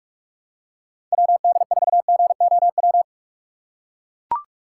Morse code, heard as a keyed tone, about 41 wpm, 700 Hz. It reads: WD4GOW
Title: USA Call Signs - 6 Characters - 6x Character Spacing - 45wpm